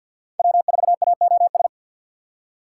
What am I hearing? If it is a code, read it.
Morse code, heard as a keyed tone, about 64 wpm, 700 Hz. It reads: W4AOS